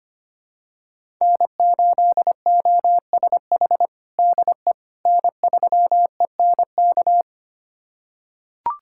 Morse code, heard as a keyed tone, about 25 wpm, 700 Hz. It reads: N8OSH DE N3ENK
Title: USA Callsign DE Callsign - 5 Characters - 25wpm